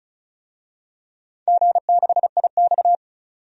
Morse code, heard as a keyed tone, about 34 wpm, 700 Hz. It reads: G6IX
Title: DXCC-100 Call Signs - 4 Characters - 35wpm